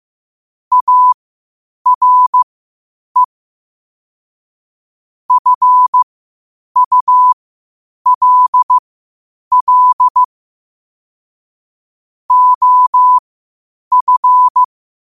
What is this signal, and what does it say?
Morse code, heard as a keyed tone, about 15 wpm, spaced as for 8 wpm, 1000 Hz. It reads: ARE FULL OF
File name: qotdfiles / 8wpm